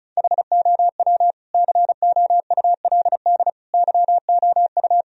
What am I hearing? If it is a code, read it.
Morse code, heard as a keyed tone, about 36 wpm, 700 Hz. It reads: HOW COULD YOU
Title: Sentences from Top 1000 Words - Review Part 2 - 35wpm